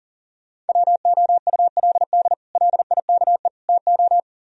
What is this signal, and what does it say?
Morse code, heard as a keyed tone, about 49 wpm, 700 Hz. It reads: WOULD LIKE TO
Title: Speed-Racing - Sets of 3 Words 40wpm